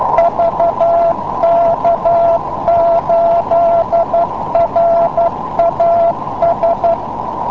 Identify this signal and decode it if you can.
Morse code, heard as a keyed tone, about 12 wpm, 680 Hz, with noise. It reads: VK8RAS